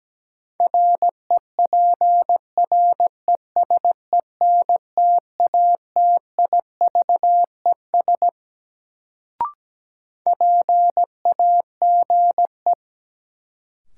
Morse code, heard as a keyed tone, about 17 wpm, 700 Hz. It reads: REPRESENTATIVES PAGE
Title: Sets of 5 Words Encore - New Word Drill - 17wpm